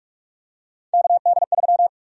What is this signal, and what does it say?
Morse code, heard as a keyed tone, about 42 wpm, 700 Hz. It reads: KD3
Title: USA Call Sign Prefixes - 45wpm